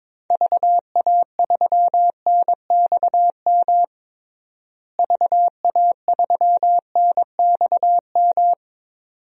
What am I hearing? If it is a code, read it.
Morse code, heard as a keyed tone, about 22 wpm, 700 Hz. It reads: VA3NXM VA3NXM
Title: DXCC-100 Call Signs - 6 Characters - Repeated 2x - 22wpm